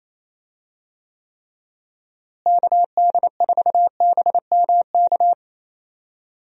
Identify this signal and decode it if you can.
Morse code, heard as a keyed tone, about 28 wpm, 700 Hz. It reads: KD4BMK
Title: USA Call Signs - 6 Characters - Repeated 3x - 28wpm